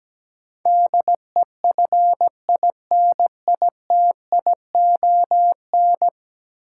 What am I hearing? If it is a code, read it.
Morse code, heard as a keyed tone, about 17 wpm, 700 Hz. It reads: DEFINITION
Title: Sentences from Top 1000 Words - Review Part 4 - New Word Drill - 17wpm